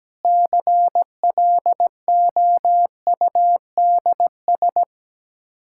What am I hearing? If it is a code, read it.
Morse code, heard as a keyed tone, about 17 wpm, 700 Hz. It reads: CLOUDS
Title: Practice QSO - Exchange 2 - Info - 17wpm